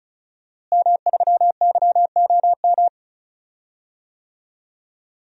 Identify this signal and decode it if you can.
Morse code, heard as a keyed tone, about 36 wpm, 700 Hz. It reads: M3YOM
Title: DXCC-100 Call Signs - 5 Characters - Repeated 3x - 35wpm